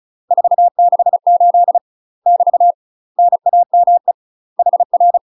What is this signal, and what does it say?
Morse code, heard as a keyed tone, about 36 wpm, 700 Hz. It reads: 468 <BT> NAME HR
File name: UK MASTER QSOs TRAINING / 35WPM / UK MASTER QSOs 0009